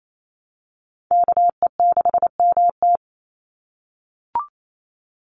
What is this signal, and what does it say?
Morse code, heard as a keyed tone, about 28 wpm, 700 Hz. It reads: KE6MT
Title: NA SOTA Chaser Callsigns 28wpm